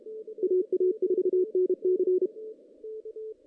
Morse code, heard as a keyed tone, about 32 wpm, 370 Hz. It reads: AA4NC
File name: 2016 JAN CW NAQP / 2016 Jan naqp k5ot - 0-5